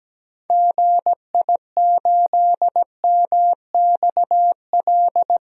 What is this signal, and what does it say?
Morse code, heard as a keyed tone, about 17 wpm, 700 Hz. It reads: GI8MXL
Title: DXCC-100 Call Signs - 6 Characters - 17wpm